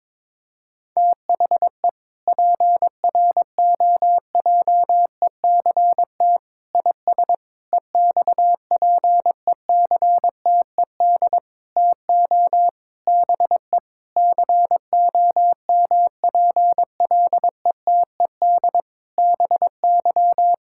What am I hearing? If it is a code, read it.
Morse code, heard as a keyed tone, about 22 wpm, 700 Hz. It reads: THE PROJECT IS EXPECTED TO BE COMPLETED BY
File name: practice / 22wpm / Sentences with 8-Word Phrases - 22wpm